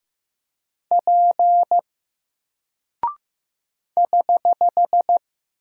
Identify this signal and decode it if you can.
Morse code, heard as a keyed tone, about 15 wpm, 700 Hz. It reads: P <HH>
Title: Single Character (Letter:Number:Punctuation:Prosign) 15wpm